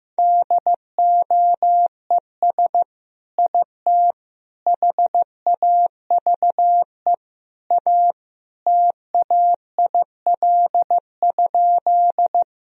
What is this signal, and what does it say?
Morse code, heard as a keyed tone, about 15 wpm, 700 Hz. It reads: DOES IT HAVE A TAIL?
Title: Sentences from Top 1000 Words - Review Part 3 - 15wpm